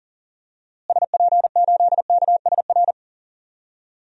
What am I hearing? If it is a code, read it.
Morse code, heard as a keyed tone, about 40 wpm, 700 Hz. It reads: SP8KSR
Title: DXCC-100 Call Signs - 6 Characters - Repeated 3x - 40wpm